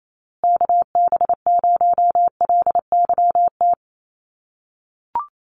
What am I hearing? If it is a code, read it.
Morse code, heard as a keyed tone, about 28 wpm, 700 Hz. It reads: KB0LYT
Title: USA Call Signs Repeated 3x 28wpm